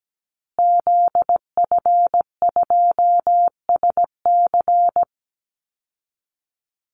Morse code, heard as a keyed tone, about 17 wpm, 700 Hz. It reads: ZF2SC
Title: Call Signs Repeated 3x - 17wpm